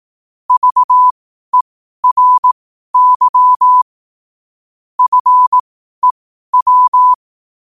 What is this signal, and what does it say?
Morse code, heard as a keyed tone, about 18 wpm, spaced as for 12 wpm, 1000 Hz. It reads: VERY FEW
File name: qotdfiles / 12wpm